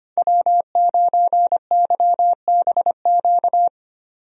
Morse code, heard as a keyed tone, about 25 wpm, 700 Hz. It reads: W9YBQ
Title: USA Call Signs - 5 Characters - 25wpm